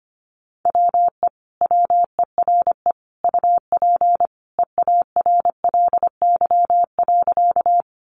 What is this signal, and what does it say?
Morse code, heard as a keyed tone, about 25 wpm, 700 Hz. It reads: WE WERE UP EARLY.
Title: Sentences from Top 400 Words 25wpm